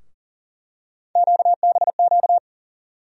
Morse code, heard as a keyed tone, about 40 wpm, 700 Hz. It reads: QBQ
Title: USA Call Sign Postfixes - 3 Letters 40wpm